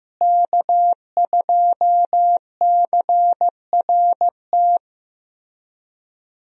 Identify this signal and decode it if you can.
Morse code, heard as a keyed tone, about 15 wpm, 700 Hz. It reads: K2CRT
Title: USA Call Signs - 5 Characters - Repeated 3x - 15wpm